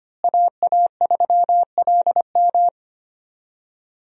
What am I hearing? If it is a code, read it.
Morse code, heard as a keyed tone, about 25 wpm, 700 Hz. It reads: AA3LM
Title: USA Call Signs - 5 Characters - Repeated 2x - 25wpm